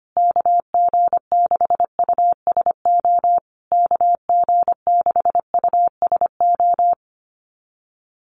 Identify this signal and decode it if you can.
Morse code, heard as a keyed tone, about 25 wpm, 700 Hz. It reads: KG6USO KG6USO
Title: USA Call Signs Repeated 3x 25wpm